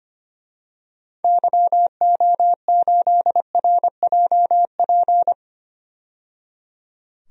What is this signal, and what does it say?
Morse code, heard as a keyed tone, about 25 wpm, 700 Hz. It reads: YO8RJP
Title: DXCC-100 Call Signs - 6 Characters - 25wpm